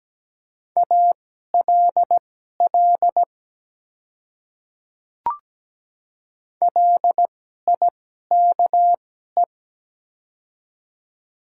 Morse code, heard as a keyed tone, about 17 wpm, spaced as for 12 wpm, 700 Hz. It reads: ALL LIKE
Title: 2x Character Spacing - Top 100 Words - 17wpm